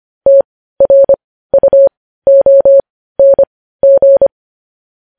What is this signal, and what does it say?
Morse code, heard as a keyed tone, about 25 wpm, spaced as for 15 wpm, 550 Hz. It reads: TRUONG